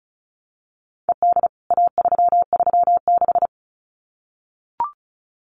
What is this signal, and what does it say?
Morse code, heard as a keyed tone, about 34 wpm, 700 Hz. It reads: ED A336
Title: CWT Exchange 35wpm